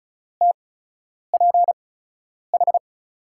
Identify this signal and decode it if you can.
Morse code, heard as a keyed tone, about 36 wpm, spaced as for 10 wpm, 700 Hz. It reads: TPH
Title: USA Call Signs - 6 Characters - 8x Character Spacing - 35wpm